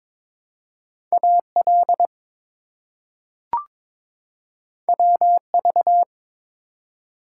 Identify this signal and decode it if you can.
Morse code, heard as a keyed tone, about 22 wpm, 700 Hz. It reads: AL WV
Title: US States and Canadian Provinces 22wpm